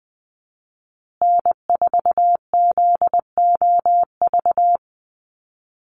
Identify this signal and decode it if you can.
Morse code, heard as a keyed tone, about 20 wpm, 700 Hz. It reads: N4ZOV
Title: USA Call Signs Repeated 3x 20wpm